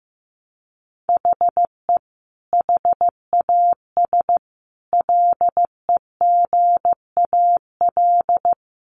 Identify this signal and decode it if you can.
Morse code, heard as a keyed tone, about 15 wpm, 700 Hz. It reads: HE HAS LEGAL
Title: Sentences from Top 1200 Words 15wpm